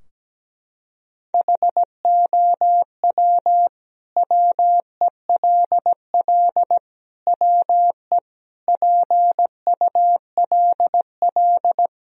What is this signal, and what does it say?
Morse code, heard as a keyed tone, about 17 wpm, 700 Hz. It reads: HOW WELL WE PULL
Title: Sentences from Top 1000 Words - Review Part 2 - 17wpm